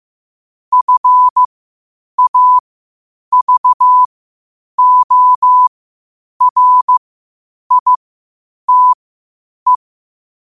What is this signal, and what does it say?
Morse code, heard as a keyed tone, about 15 wpm, spaced as for 8 wpm, 1000 Hz. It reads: FAVORITE
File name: qotdfiles / 8wpm